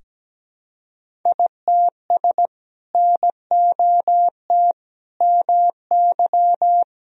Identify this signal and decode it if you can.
Morse code, heard as a keyed tone, about 17 wpm, 700 Hz. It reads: ITS NOT MY
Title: Sentences from Top 1000 Words - Review Part 1 - 17wpm